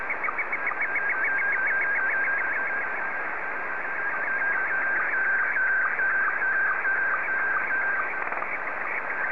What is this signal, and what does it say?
SSTV (slow-scan television), Scottie 1.